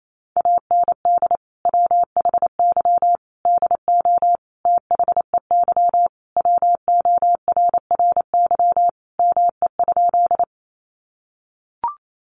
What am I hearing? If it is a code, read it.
Morse code, heard as a keyed tone, about 28 wpm, 700 Hz. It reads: AND WHY DO THEY WORRY ME?
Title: Sentences from Top 1100 Words 28wpm